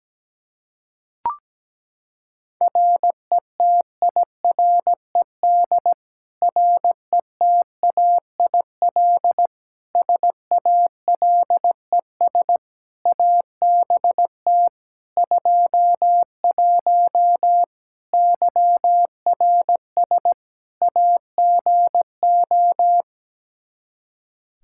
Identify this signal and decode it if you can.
Morse code, heard as a keyed tone, about 17 wpm, 700 Hz. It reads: RETIRED RETAIL SALES ABT 21 YRS AGO
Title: Practice QSO - Exchange 3 - More Info - 17wpm